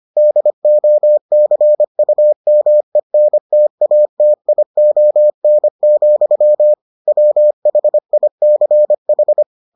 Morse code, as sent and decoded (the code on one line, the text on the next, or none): -.. --- -.-. ..- -- . -. - .- - .. --- -. --..-- / .-- .... .. -.-. ....
DOCUMENTATION, WHICH